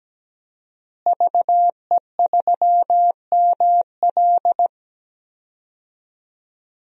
Morse code, heard as a keyed tone, about 17 wpm, 700 Hz. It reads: VE3ML
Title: DXCC-100 Call Signs - 5 Characters - Repeated 3x - 17wpm